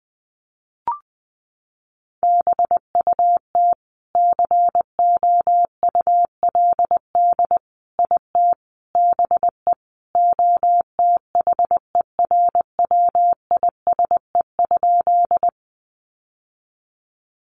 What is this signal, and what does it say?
Morse code, heard as a keyed tone, about 20 wpm, 700 Hz. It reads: BUT COULD IT BE OTHERWISE?
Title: Sentences from Top 1200 Words 20wpm